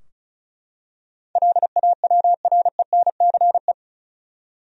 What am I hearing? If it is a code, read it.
Morse code, heard as a keyed tone, about 36 wpm, 700 Hz. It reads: LAWRENCE
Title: USA Common Names - 35wpm